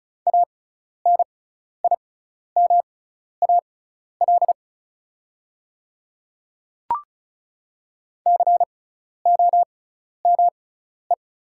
Morse code, heard as a keyed tone, about 34 wpm, spaced as for 12 wpm, 700 Hz. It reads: ANIMAL COME
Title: 6x Character Spacing - Top 200 Words - 35wpm